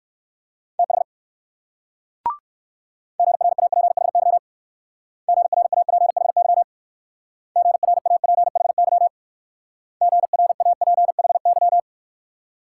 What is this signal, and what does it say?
Morse code, heard as a keyed tone, about 56 wpm, 690 Hz. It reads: TH GRAPHY GRAPHY GRAPHY GRAPHY